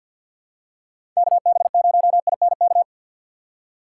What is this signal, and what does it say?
Morse code, heard as a keyed tone, about 51 wpm, 700 Hz. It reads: KB0INK
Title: USA Call Signs - 6 Characters - 6x Character Spacing - 50wpm